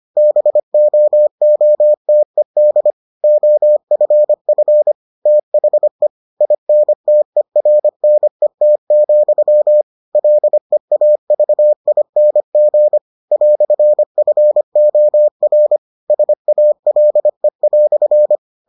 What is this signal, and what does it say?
Morse code, heard as a keyed tone, about 25 wpm, 600 Hz. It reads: BOOTED OFF THE INTERNET, LEAVING "FOR SALE"